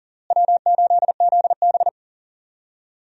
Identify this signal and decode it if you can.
Morse code, heard as a keyed tone, about 40 wpm, 700 Hz. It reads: W8ZB
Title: USA Call Signs - 4 Characters - Repeated 3x - 40wpm